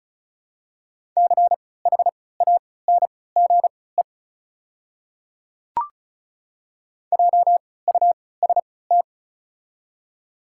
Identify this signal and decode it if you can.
Morse code, heard as a keyed tone, about 36 wpm, spaced as for 20 wpm, 700 Hz. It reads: CHANGE JUST
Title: 3x Character Spacing - Top 200 Words - 35wpm